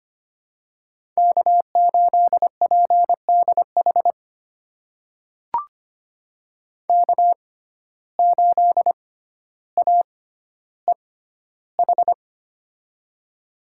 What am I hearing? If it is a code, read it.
Morse code, heard as a keyed tone, about 25 wpm, spaced as for 13 wpm, 700 Hz. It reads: K8PDH K8AEH
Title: USA Call Signs - 5 Characters - 6x Character Spacing - 25wpm